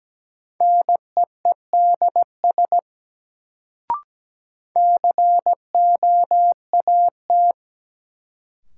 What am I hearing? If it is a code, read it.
Morse code, heard as a keyed tone, about 17 wpm, 700 Hz. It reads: NEEDS COAT
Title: Sentences from Top 1600 Words - New Word Drill Alt - 17wpm